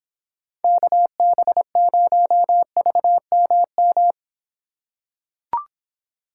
Morse code, heard as a keyed tone, about 25 wpm, 700 Hz. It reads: KB0VMM